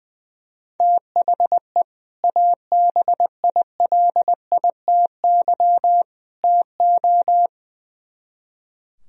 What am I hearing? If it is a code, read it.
Morse code, heard as a keyed tone, about 20 wpm, 700 Hz. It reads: THE ABILITY TO
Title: Speed-Racing - Sets of 3 Words 20wpm